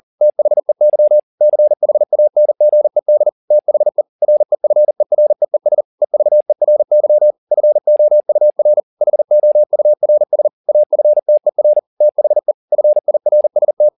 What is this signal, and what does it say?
Morse code, heard as a keyed tone, about 40 wpm, 600 Hz. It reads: THEY CHANGED THE REFEREES EVERY FOUR HOURS AFTER THE FIRST
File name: the old man and the sea 2 40wpm